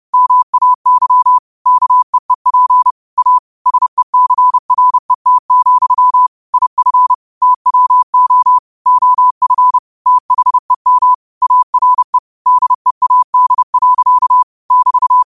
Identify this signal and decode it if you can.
Morse code, heard as a keyed tone, about 30 wpm, 1000 Hz. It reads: MAY KEEP A SECRET, IF TWO OF THEM ARE DEAD. <BT>
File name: qotdfiles / 30wpm